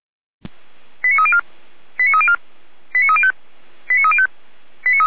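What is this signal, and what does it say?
Five-tone selective call (ZVEI), heard as quick runs of five tones, tones of about 70 ms. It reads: 89304 89304 89305 89305
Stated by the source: ZVEI five-tone call received by SDR, Feuerwehr Orlamünde fire brigade.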